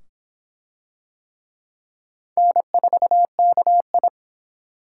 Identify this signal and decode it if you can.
Morse code, heard as a keyed tone, about 23 wpm, 700 Hz. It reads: N4KI